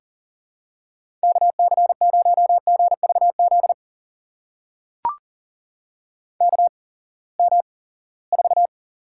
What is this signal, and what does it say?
Morse code, heard as a keyed tone, about 40 wpm, spaced as for 23 wpm, 700 Hz. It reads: KC0GVZ KM4
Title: USA Call Signs - 6 Characters - 8x Character Spacing - 40wpm